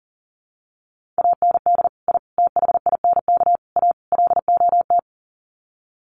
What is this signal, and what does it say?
Morse code, heard as a keyed tone, about 40 wpm, 700 Hz. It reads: AND I THINK A LOT
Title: Sets of 5 Words Encore2 40wpm